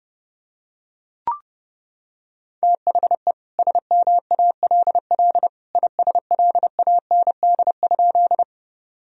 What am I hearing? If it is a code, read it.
Morse code, heard as a keyed tone, about 30 wpm, 700 Hz. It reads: THE SMALL ISLAND?
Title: Sentences from Top 1000 Words - Review Part 3 - 30wpm